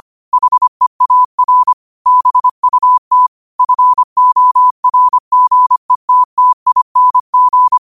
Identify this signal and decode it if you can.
Morse code, heard as a keyed tone, about 25 wpm, 1000 Hz. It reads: HEAR BUT FORGETTING
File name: qotdfiles / 25wpm